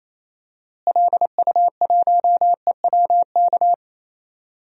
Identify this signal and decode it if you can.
Morse code, heard as a keyed tone, about 27 wpm, 700 Hz. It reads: LU1EWK
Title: DXCC-100 Call Signs - 6 Characters - Repeated 3x - 28wpm